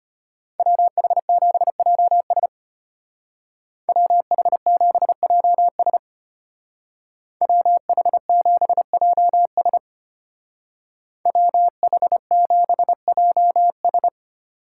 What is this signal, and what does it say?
Morse code, heard as a keyed tone, about 31 wpm, 700 Hz. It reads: WH7JS WH7JS WH7JS WH7JS